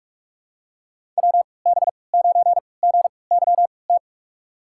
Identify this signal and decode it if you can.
Morse code, heard as a keyed tone, about 41 wpm, spaced as for 25 wpm, 700 Hz. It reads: WB9GYT
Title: USA Call Signs - 6 Characters - 3x Character Spacing - 45wpm